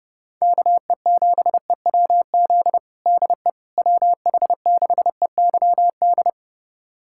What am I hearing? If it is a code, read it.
Morse code, heard as a keyed tone, about 30 wpm, 700 Hz. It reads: KE7EWZ DE WH6EYD
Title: USA Callsign DE Callsign 30wpm